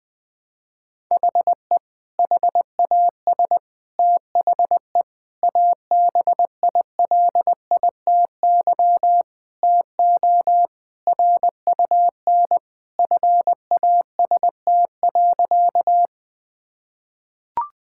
Morse code, heard as a keyed tone, about 20 wpm, 700 Hz. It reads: HE HAS THE ABILITY TO RUN FAST.
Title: Sentences with 3-Word Phrases - 20wpm